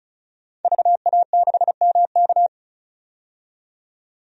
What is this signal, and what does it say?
Morse code, heard as a keyed tone, about 36 wpm, 700 Hz. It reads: VA6MK
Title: DXCC-100 Call Signs - 5 Characters - 35wpm